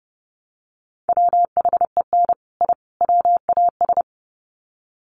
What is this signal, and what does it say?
Morse code, heard as a keyed tone, about 30 wpm, 700 Hz. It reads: WHEN I WAS
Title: Sets of 3 Words 30wpm